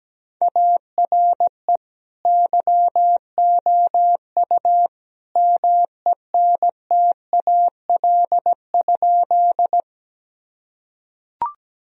Morse code, heard as a keyed tone, about 17 wpm, 700 Hz. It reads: ARE YOU MENTAL?
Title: Sentences from Top 1000 Words - Review Part 4 - 17wpm